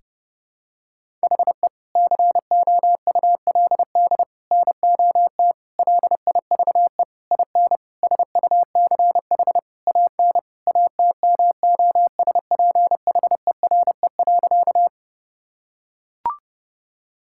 Morse code, heard as a keyed tone, about 30 wpm, 700 Hz. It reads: HE COULD NOT LIVE IN SUCH AN ATMOSPHERE.
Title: Sentences from Top 1600 Words 30wpm